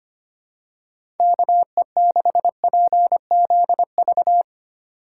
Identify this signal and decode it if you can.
Morse code, heard as a keyed tone, about 25 wpm, 700 Hz. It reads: KE6PZV
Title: USA Call Signs - 6 Characters - 2x Character Spacing - 25wpm